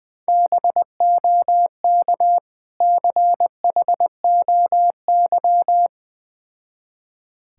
Morse code, heard as a keyed tone, about 20 wpm, 700 Hz. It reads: BOK CHOY
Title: Difficult Words with Hints - 20wpm